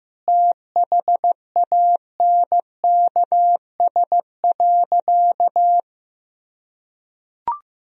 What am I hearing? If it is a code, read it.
Morse code, heard as a keyed tone, about 15 wpm, 700 Hz. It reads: THANKS.
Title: Sentences from Top 1000 Words - Review Part 3 - 15wpm